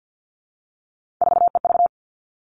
Morse code, heard as a keyed tone, about 50 wpm, 700 Hz. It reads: 4EV
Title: Three Challenging Letter-Number Combinations 50wpm